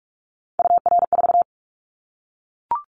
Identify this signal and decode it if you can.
Morse code, heard as a keyed tone, about 44 wpm, 700 Hz. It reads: UR4
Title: Callsign Prefixes for Top 25 DXCC - Repeated 3x - 45wpm